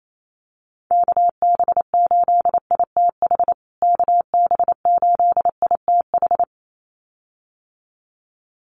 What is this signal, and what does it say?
Morse code, heard as a keyed tone, about 28 wpm, 700 Hz. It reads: KB8ITH KB8ITH
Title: USA Call Signs Repeated 2x 28wpm